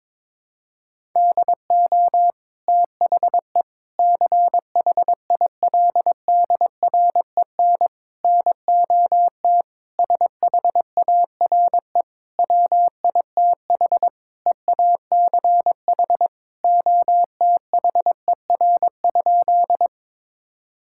Morse code, heard as a keyed tone, about 22 wpm, 700 Hz. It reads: DO THE CHILDREN NOT SHARE WITH EACH OTHER?
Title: Sentences from Top 1000 Words - Review Part 6 - 22wpm